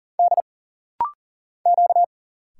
Morse code, heard as a keyed tone, about 40 wpm, 700 Hz. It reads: D Q
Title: Single Letter-Number Alt - Rapid-Fire 40wpm